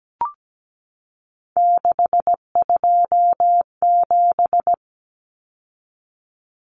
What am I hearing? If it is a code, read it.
Morse code, heard as a keyed tone, about 17 wpm, 700 Hz. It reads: 627
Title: Numbers - 3-Digit 17wpm